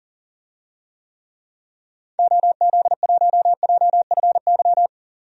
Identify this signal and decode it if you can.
Morse code, heard as a keyed tone, about 37 wpm, 700 Hz. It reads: OZ1JFY
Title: DXCC-100 Call Signs - 6 Characters - 40wpm